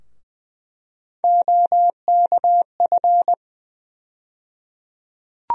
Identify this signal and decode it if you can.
Morse code, heard as a keyed tone, about 20 wpm, 700 Hz. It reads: OKF
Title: USA Call Sign Postfixes - 3 Letters 20wpm